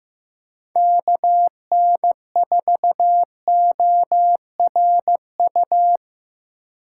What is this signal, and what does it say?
Morse code, heard as a keyed tone, about 15 wpm, 700 Hz. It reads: KN4ORU
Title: USA Call Signs - 4x Character Spacing - 15wpm